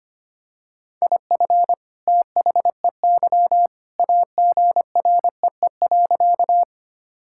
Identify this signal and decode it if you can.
Morse code, heard as a keyed tone, about 25 wpm, 700 Hz. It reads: IF THEY AGREE.
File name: practice / 25wpm / Sentences with 2-Word Phrases - 25wpm